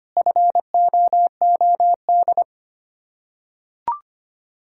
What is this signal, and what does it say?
Morse code, heard as a keyed tone, about 25 wpm, 700 Hz. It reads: FOOD
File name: practice / 25wpm / Difficult Words with Hints - 25wpm